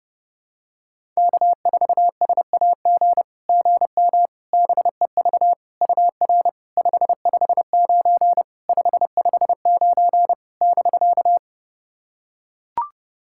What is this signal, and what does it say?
Morse code, heard as a keyed tone, about 30 wpm, 700 Hz. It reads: K4SAG GM BEV UR 559 559 <BK>
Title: SOTA QSO Practice 30wpm